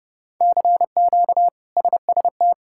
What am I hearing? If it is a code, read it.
Morse code, heard as a keyed tone, about 32 wpm, 700 Hz. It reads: CQ SST
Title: K1USN SST QSO Practice - 30wpm